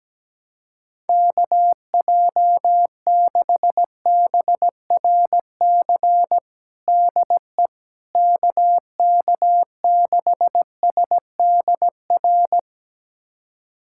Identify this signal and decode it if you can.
Morse code, heard as a keyed tone, about 17 wpm, 700 Hz. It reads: KJ6BRC DE KK6SDR
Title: USA Callsign DE Callsign - 6 Characters - 17wpm